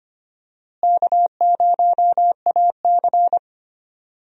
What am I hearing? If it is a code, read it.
Morse code, heard as a keyed tone, about 25 wpm, 700 Hz. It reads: K0AC
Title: USA Call Signs - 4 Characters - Repeated 2x - 25wpm